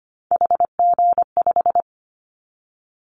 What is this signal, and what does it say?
Morse code, heard as a keyed tone, about 25 wpm, 700 Hz. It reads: HG5
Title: Callsign Prefixes for Top 50 DXCC - 25wpm